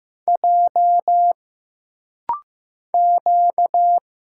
Morse code, heard as a keyed tone, about 15 wpm, 700 Hz. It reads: J Q
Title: Single Letters - Rapid-Fire 15wpm